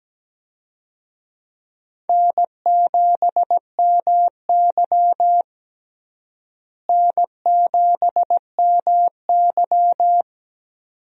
Morse code, heard as a keyed tone, about 17 wpm, 700 Hz. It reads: N7MY N7MY
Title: USA Call Signs - 4 Characters - Repeated 2x - 17wpm